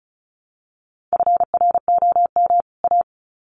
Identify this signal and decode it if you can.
Morse code, heard as a keyed tone, about 34 wpm, 700 Hz. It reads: FROM A
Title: Sets of 2 Words 35wpm